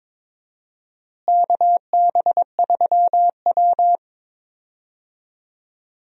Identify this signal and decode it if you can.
Morse code, heard as a keyed tone, about 22 wpm, 700 Hz. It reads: KB3W
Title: USA Call Signs - 4 Characters - Repeated 2x - 22wpm